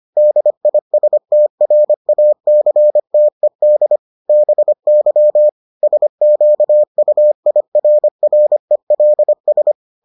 Morse code, heard as a keyed tone, about 25 wpm, 600 Hz. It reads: DISTRACTED BY SQUIRRELS